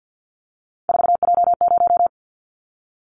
Morse code, heard as a keyed tone, about 50 wpm, 690 Hz. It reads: 4J0
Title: Callsign Prefixes for Top 100 DXCC - 50wpm